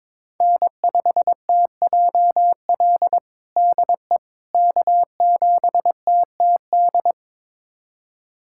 Morse code, heard as a keyed tone, about 21 wpm, 700 Hz. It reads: N5TJL DE K7TTD
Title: USA Callsign DE Callsign - 5 Characters - 22wpm